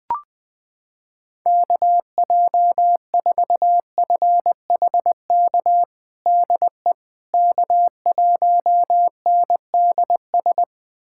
Morse code, heard as a keyed tone, about 20 wpm, 690 Hz. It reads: KJ4FHK DE K1NDS
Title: USA Callsign DE Callsign 20wpm